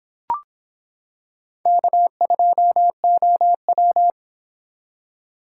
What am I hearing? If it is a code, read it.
Morse code, heard as a keyed tone, about 26 wpm, 700 Hz. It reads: K2OW